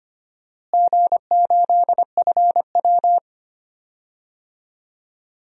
Morse code, heard as a keyed tone, about 25 wpm, 700 Hz. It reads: G8FW
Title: DXCC-100 Call Signs - 4 Characters - Repeated 3x - 25wpm